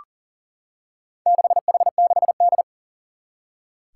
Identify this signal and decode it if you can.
Morse code, heard as a keyed tone, about 39 wpm, 700 Hz. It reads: 6H6D
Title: Four Challenging Letter-Number Combinations 40wpm